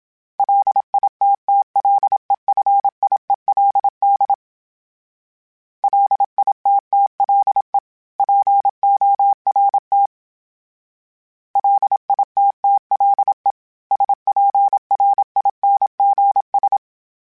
Morse code, heard as a keyed tone, about 27 wpm, 800 Hz. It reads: LITTLEFIELD LITTLE PORT LITTLE SPRINGS
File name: fromSMs / FromSMs mp3s / Cities@25